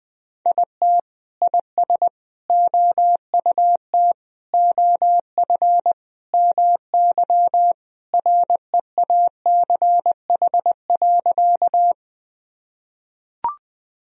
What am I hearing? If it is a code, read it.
Morse code, heard as a keyed tone, about 21 wpm, 700 Hz. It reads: IT IS OUT OF MY REACH.
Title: Sentences from Top 1000 Words - Review Part 2 - 20wpm